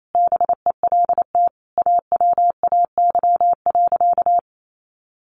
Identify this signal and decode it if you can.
Morse code, heard as a keyed tone, about 28 wpm, 700 Hz. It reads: BELT AWAY.
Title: Sentences from Top 1400 Words 28wpm